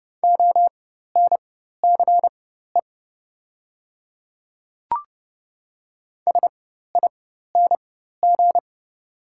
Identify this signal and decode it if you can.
Morse code, heard as a keyed tone, about 30 wpm, spaced as for 14 wpm, 700 Hz. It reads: ONCE SING
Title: 4x Character Spacing - Top 500 Words - 30wpm